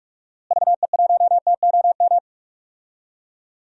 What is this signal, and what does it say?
Morse code, heard as a keyed tone, about 46 wpm, 700 Hz. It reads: VE1TOM
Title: DXCC-100 Call Signs - 6 Characters - Repeated 3x - 45wpm